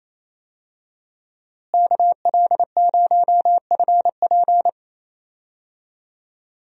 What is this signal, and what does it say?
Morse code, heard as a keyed tone, about 28 wpm, 700 Hz. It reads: KL0FP
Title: USA Call Signs - 5 Characters - Repeated 2x - 28wpm